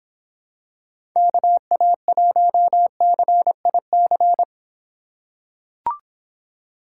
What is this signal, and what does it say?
Morse code, heard as a keyed tone, about 26 wpm, 700 Hz. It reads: KA1CIC